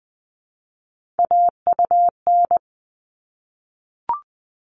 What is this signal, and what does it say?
Morse code, heard as a keyed tone, about 20 wpm, 700 Hz. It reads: AUN
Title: All Cut Numbers - 3-Digit 20wpm